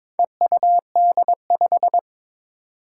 Morse code, heard as a keyed tone, about 22 wpm, 700 Hz. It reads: EUD5
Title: Four Challenging Letter-Number Combinations 22wpm